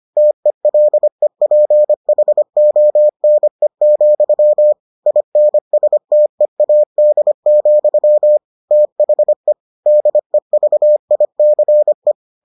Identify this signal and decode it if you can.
Morse code, heard as a keyed tone, about 25 wpm, 600 Hz. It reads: TELEPHONE, INSTEAD, THE DEVICE